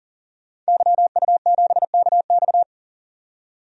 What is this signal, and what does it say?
Morse code, heard as a keyed tone, about 38 wpm, 700 Hz. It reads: YU7KX
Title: DXCC-100 Call Signs - 5 Characters - 40wpm